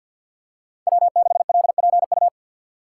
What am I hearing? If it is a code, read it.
Morse code, heard as a keyed tone, about 51 wpm, 700 Hz. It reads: W6LPU
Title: USA Call Signs - 5 Characters - 2x Character Spacing - 50wpm